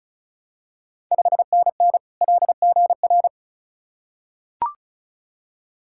Morse code, heard as a keyed tone, about 35 wpm, 690 Hz. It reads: 5NN LGR
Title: Ontario QSO Party Exchange 35wpm